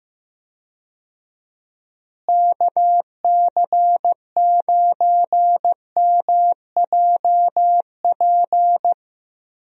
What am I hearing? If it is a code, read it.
Morse code, heard as a keyed tone, about 15 wpm, 700 Hz. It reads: KC9MJP
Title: USA Call Signs - 6 Characters - Repeated 2x - 15wpm